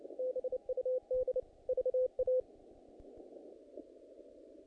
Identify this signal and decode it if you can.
Morse code, heard as a keyed tone, about 29 wpm, 510 Hz. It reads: BUD VA
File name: contests / naqp-cw-k5tr-op-k5pi- -2000